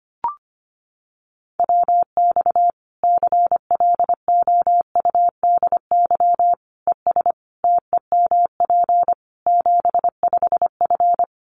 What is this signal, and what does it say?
Morse code, heard as a keyed tone, about 25 wpm, 700 Hz. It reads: WX CLOUDY ES TEMP 75F
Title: Common QSO Phrases 25wpm